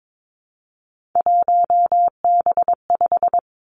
Morse code, heard as a keyed tone, about 22 wpm, 700 Hz. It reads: 1B5
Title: Three Challenging Letter-Number Combinations 22wpm